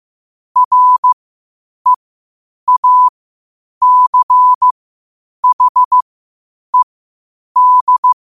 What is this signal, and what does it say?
Morse code, heard as a keyed tone, about 15 wpm, spaced as for 8 wpm, 1000 Hz. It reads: REACHED
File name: qotdfiles / 8wpm